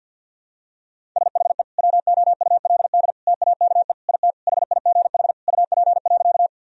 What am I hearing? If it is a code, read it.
Morse code, heard as a keyed tone, about 52 wpm, 700 Hz. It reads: SHE WOULD TAKE IT HIGH UP.
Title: Sentences from Top 1000 Words - Review Part 1 - 50wpm